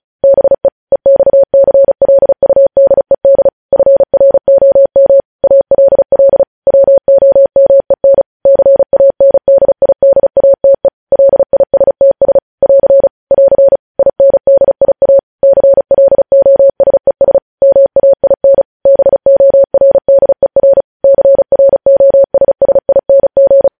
Morse code, heard as a keyed tone, about 36 wpm, 550 Hz. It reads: BE EXCLUDED FROM ALL WOMEN CANDIDATE LISTS <AR> <AR> INDIA CLOSES MAIN BORDER CROSSING
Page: news headlines in morse code at 35 wpm